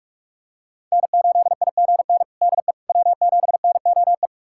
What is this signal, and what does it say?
Morse code, heard as a keyed tone, about 44 wpm, 700 Hz. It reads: N8IGN DE W7NOE
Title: USA Callsign DE Callsign - 5 Characters - 45wpm